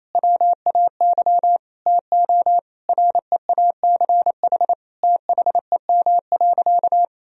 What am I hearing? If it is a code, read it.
Morse code, heard as a keyed tone, about 27 wpm, 700 Hz. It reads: WAY TO REACH THEM.
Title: Sentences from Top 1000 Words - Review Part 2 - 28wpm